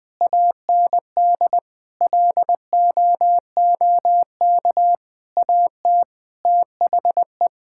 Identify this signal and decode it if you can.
Morse code, heard as a keyed tone, about 20 wpm, 700 Hz. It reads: AND LOOK AT THE
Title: Sets of 8 Words - 20wpm